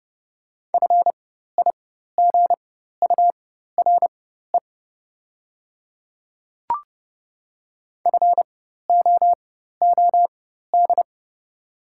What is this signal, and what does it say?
Morse code, heard as a keyed tone, about 31 wpm, spaced as for 14 wpm, 700 Hz. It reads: FIGURE FOOD
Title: 4x Character Spacing - Top 500 Words - 30wpm